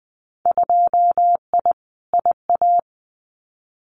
Morse code, heard as a keyed tone, about 20 wpm, 700 Hz. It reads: 2I IA
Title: Winter Field Day Exchanges Encore 20wpm